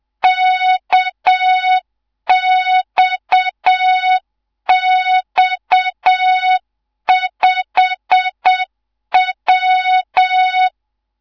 Morse code, heard as a keyed tone, about 7 wpm, 3700 Hz, with noise. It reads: KXX5W